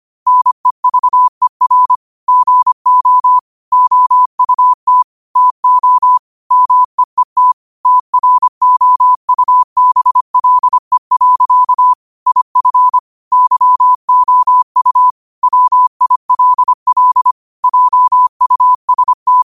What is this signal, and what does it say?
Morse code, heard as a keyed tone, about 25 wpm, 1000 Hz. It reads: NEVER GO OUT TO MEET TROUBLE. IF YOU WILL JUST